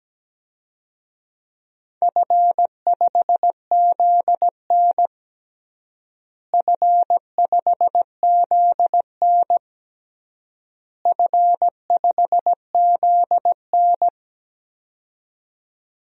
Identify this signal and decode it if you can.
Morse code, heard as a keyed tone, about 17 wpm, 700 Hz. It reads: F5ZN F5ZN F5ZN
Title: DXCC-100 Call Signs - 4 Characters - Repeated 3x - 17wpm